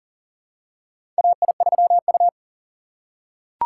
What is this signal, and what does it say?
Morse code, heard as a keyed tone, about 40 wpm, 700 Hz. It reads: AI3U
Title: USA Call Signs - 4 Characters - 4x Character Spacing - 40wpm